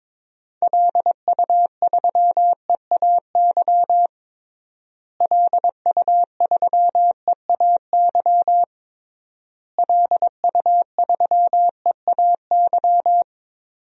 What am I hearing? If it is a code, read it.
Morse code, heard as a keyed tone, about 22 wpm, 700 Hz. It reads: LU3EAY LU3EAY LU3EAY
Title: DXCC-100 Call Signs - 6 Characters - Repeated 3x - 22wpm